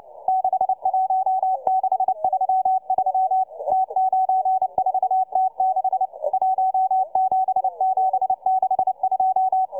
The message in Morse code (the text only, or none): 61633 T94T6 17762